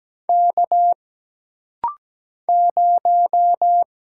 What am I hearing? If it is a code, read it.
Morse code, heard as a keyed tone, about 17 wpm, 700 Hz. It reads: K 0
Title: Single Letter-Number - Rapid-Fire 17wpm